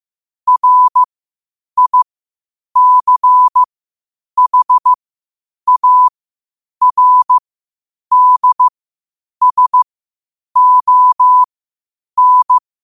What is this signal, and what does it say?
Morse code, heard as a keyed tone, about 15 wpm, spaced as for 8 wpm, 1000 Hz. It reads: RICHARDSON
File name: qotdfiles / 8wpm